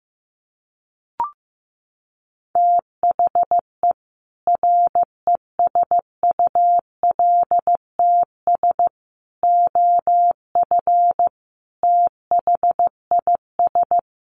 Morse code, heard as a keyed tone, about 15 wpm, 700 Hz. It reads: THE RESULTS OF THIS
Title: Sets of 5 Words 15wpm